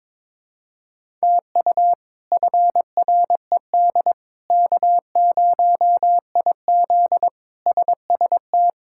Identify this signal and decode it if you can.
Morse code, heard as a keyed tone, about 22 wpm, 700 Hz. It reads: TU FRED K0IZ SST
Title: K1USN SST QSO Practice - 22wpm